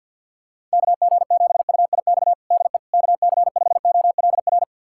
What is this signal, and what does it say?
Morse code, heard as a keyed tone, about 52 wpm, 700 Hz. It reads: KG7UIX DE KC5OLR
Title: USA Callsign DE Callsign - 6 Characters - 50wpm